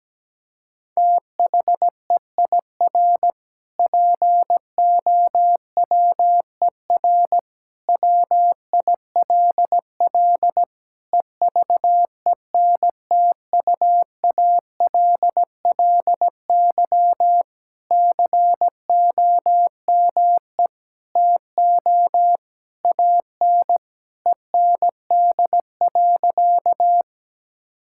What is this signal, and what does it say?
Morse code, heard as a keyed tone, about 17 wpm, 700 Hz. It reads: THEIR POWER WILL EVENTUALLY COME TO AN END.
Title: Sentences from Top 1000 Words - Review Part 6 - 17wpm